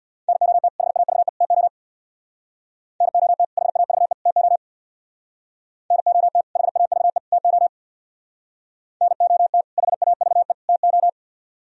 Morse code, heard as a keyed tone, about 60 wpm, 700 Hz. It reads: NOT HAVE TO NOT HAVE TO NOT HAVE TO NOT HAVE TO
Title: Speed-Racing - Sets of 3 Words 50wpm